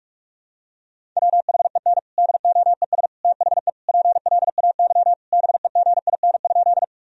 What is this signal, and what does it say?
Morse code, heard as a keyed tone, about 46 wpm, 700 Hz. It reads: WHEN DOES THE PLAY BEGIN?
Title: Sentences from Top 1000 Words - Review Part 1 - 45wpm